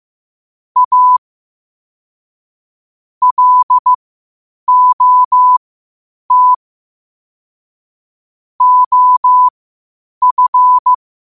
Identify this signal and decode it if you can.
Morse code, heard as a keyed tone, about 15 wpm, spaced as for 8 wpm, 1000 Hz. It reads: A LOT OF
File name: qotdfiles / 8wpm